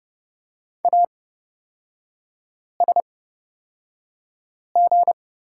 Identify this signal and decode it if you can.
Morse code, heard as a keyed tone, about 30 wpm, spaced as for 5 wpm, 700 Hz. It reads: ASG
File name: Lesson 37e - Letter x - Three Characters 30wpm